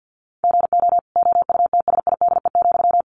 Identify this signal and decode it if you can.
Morse code, heard as a keyed tone, about 50 wpm, 700 Hz. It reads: GO OUTSIDE,
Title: Sentences from Top 900 Words 50wpm